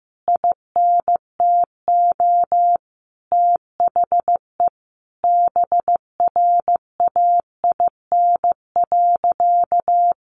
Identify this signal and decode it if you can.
Morse code, heard as a keyed tone, about 15 wpm, 700 Hz. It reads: INTO THE BRAIN.
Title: A Sound Of Thunder 15wpm